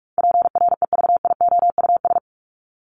Morse code, heard as a keyed tone, about 44 wpm, 700 Hz. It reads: PREVIOUS
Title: Top 1251-1500 Words 45wpm